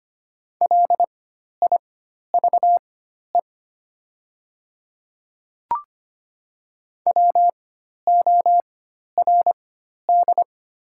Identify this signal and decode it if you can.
Morse code, heard as a keyed tone, about 25 wpm, spaced as for 12 wpm, 690 Hz. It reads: LIVE WORD
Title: 4x Character Spacing - Top 200 Words - 25wpm